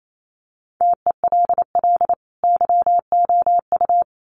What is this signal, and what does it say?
Morse code, heard as a keyed tone, about 28 wpm, 700 Hz. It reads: TELL YOU
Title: Sets of 2 Words 28wpm